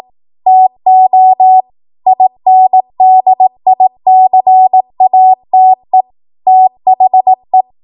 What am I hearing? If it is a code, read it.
Morse code, heard as a keyed tone, about 18 wpm, 750 Hz. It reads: TO INDICATE THE
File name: Morse / 18WPM